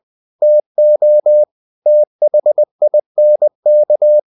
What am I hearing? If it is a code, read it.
Morse code, heard as a keyed tone, about 20 wpm, 600 Hz. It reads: TO THINK
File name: the old man and the sea part 2 20wpm